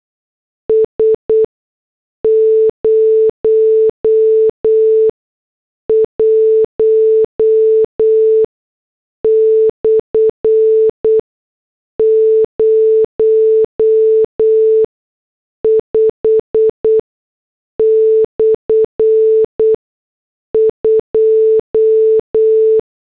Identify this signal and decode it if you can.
Morse code, heard as a keyed tone, about 8 wpm, spaced as for 6 wpm, 440 Hz. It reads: S 01/05/2
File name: morse / headlines